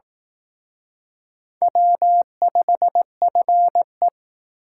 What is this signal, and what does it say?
Morse code, heard as a keyed tone, about 18 wpm, 700 Hz. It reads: W5FE
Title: Speed Racing - USA Call Signs 15wpm